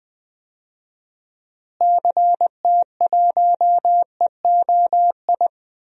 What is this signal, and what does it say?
Morse code, heard as a keyed tone, about 20 wpm, 700 Hz. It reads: CT1EOI
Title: DXCC-100 Call Signs - 6 Characters - Repeated 3x - 20wpm